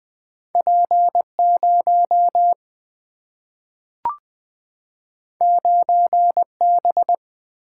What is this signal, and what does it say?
Morse code, heard as a keyed tone, about 20 wpm, 700 Hz. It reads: P0 9B
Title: Single-Letter Single-Number 20wpm